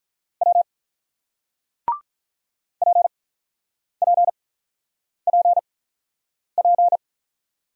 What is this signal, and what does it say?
Morse code, heard as a keyed tone, about 46 wpm, spaced as for 9 wpm, 700 Hz. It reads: W PPPP